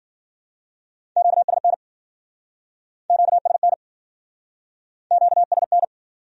Morse code, heard as a keyed tone, about 57 wpm, 700 Hz. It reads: QSN QSN QSN